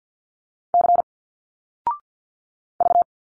Morse code, heard as a keyed tone, about 51 wpm, 700 Hz. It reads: C V
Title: Single Letters 50wpm